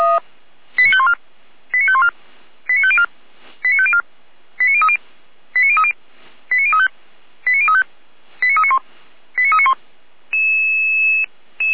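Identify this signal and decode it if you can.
Five-tone selective call (ZVEI), heard as quick runs of five tones, tones of about 70 ms. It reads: 89514 89514 89504 89504 89030 89030 89035 89035 89391 89391
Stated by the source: Five-tone ZVEI selective call received by SDR.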